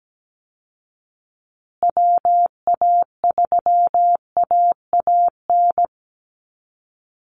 Morse code, heard as a keyed tone, about 17 wpm, 700 Hz. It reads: WA3AAN
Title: Call Signs Repeated 2x - 17wpm